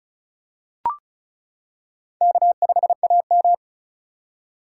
Morse code, heard as a keyed tone, about 36 wpm, 700 Hz. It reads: K5AM
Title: K1USN SST QSO Practice - 35wpm